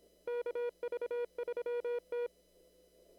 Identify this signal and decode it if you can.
Morse code, heard as a keyed tone, about 26 wpm, 500 Hz. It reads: KV3T
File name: CW / kt5j wpx cw-op-k5pi